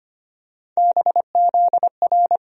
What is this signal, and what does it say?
Morse code, heard as a keyed tone, about 25 wpm, 700 Hz. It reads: BZR
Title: USA Call Sign Postfixes - 3 Letters 25wpm